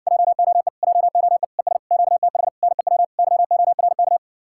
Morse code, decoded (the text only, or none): JOE JOE ES QTH NEW YORK